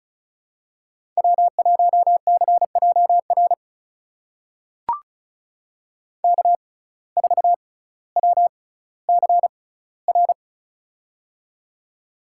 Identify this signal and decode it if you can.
Morse code, heard as a keyed tone, about 34 wpm, spaced as for 18 wpm, 700 Hz. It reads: W1CJR K4WCR
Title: USA Call Signs - 5 Characters - 6x Character Spacing - 35wpm